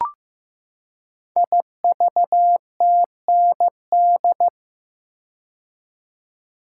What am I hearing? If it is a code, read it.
Morse code, heard as a keyed tone, about 15 wpm, 700 Hz. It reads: IVTND
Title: Code Groups - 5 Letters - 15wpm